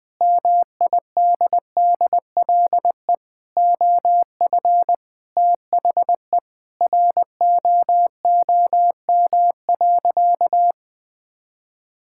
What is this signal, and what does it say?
Morse code, heard as a keyed tone, about 21 wpm, 700 Hz. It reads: MIDDLE OF THE ROOM.
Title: Sentences from Top 1000 Words - Review Part 6 - 20wpm